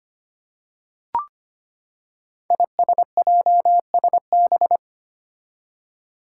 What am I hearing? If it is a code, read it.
Morse code, heard as a keyed tone, about 25 wpm, 700 Hz. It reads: ISJSB
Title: Five Challenging Letter-Number Combinations 25wpm